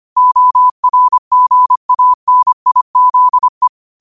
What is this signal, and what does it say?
Morse code, heard as a keyed tone, about 25 wpm, 1000 Hz. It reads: ORGANIZE
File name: qotdfiles / 25wpm